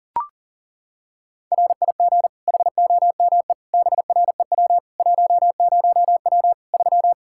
Morse code, heard as a keyed tone, about 40 wpm, 700 Hz. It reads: RIG HOME BREW 10W 3
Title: K1USN SST QSO Practice 40wpm